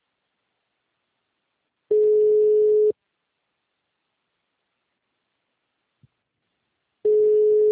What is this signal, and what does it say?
Ringback tone (Europe, 425 Hz).
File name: out- -2001- - - .54268